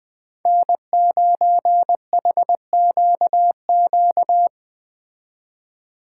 Morse code, heard as a keyed tone, about 20 wpm, 700 Hz. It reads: N9HQQ
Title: Speed Racing - USA 1x3 Call Signs 17wpm